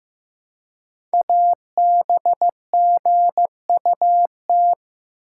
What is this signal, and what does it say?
Morse code, heard as a keyed tone, about 15 wpm, 690 Hz. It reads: ABGUT
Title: Code Groups - 5 Letters - 15wpm